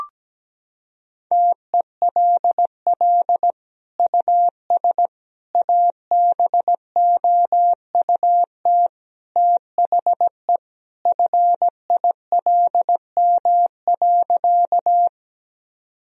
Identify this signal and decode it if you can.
Morse code, heard as a keyed tone, about 17 wpm, 700 Hz. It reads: TELL US ABOUT THE FILM.
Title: Sentences from Top 1000 Words - Review Part 3 - 17wpm